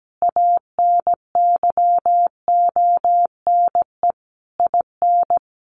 Morse code, heard as a keyed tone, about 17 wpm, 700 Hz. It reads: ANYONE IN
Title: Sentences from Top 1200 Words 17wpm